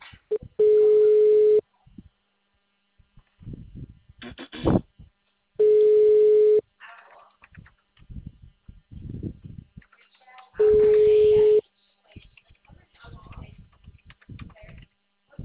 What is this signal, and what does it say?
Ringback tone (Europe, 425 Hz), with speech.
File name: out- -2014- - - .52036